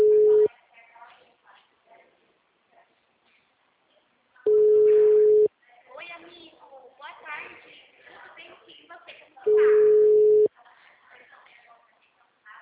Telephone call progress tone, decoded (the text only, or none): ringback tone